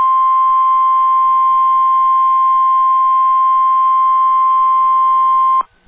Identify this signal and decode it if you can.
Emergency Alert System (SAME) tones (the 1050 Hz attention tone). It is heard from 1.1 to 5.4 s.